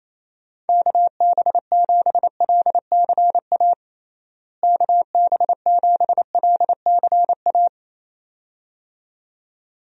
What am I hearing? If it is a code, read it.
Morse code, heard as a keyed tone, about 28 wpm, 700 Hz. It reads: KB7LCA KB7LCA
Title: USA Call Signs - 6 Characters - Repeated 2x - 28wpm